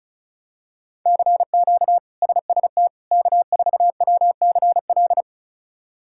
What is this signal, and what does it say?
Morse code, heard as a keyed tone, about 36 wpm, 700 Hz. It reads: CQ SST K4WCL
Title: K1USN SST QSO Practice - 35wpm